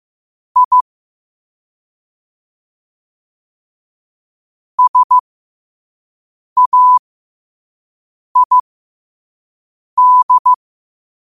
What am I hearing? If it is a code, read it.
Morse code, heard as a keyed tone, about 15 wpm, spaced as for 5 wpm, 1000 Hz. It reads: I SAID
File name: qotdfiles / 5wpm